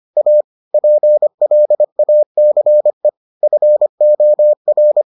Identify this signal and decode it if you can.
Morse code, heard as a keyed tone, about 25 wpm, 600 Hz. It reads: A PLACE FOR